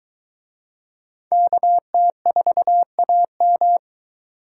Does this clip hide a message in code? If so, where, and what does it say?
Yes: Morse code from 1.3 to 3.8 s, heard as a keyed tone, about 23 wpm, 700 Hz. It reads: KT4AM